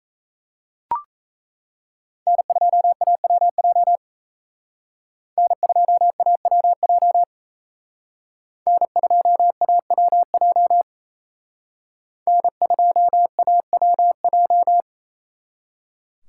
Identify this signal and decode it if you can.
Morse code, heard as a keyed tone, about 34 wpm, 700 Hz. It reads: N2AWJ N2AWJ N2AWJ N2AWJ
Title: Speed Racing - USA 1x3 Call Signs 28wpm